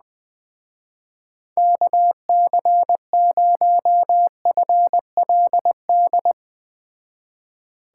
Morse code, heard as a keyed tone, about 20 wpm, 700 Hz. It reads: KC0FLD
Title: Speed Racing - USA 2x3 Call Signs 17wpm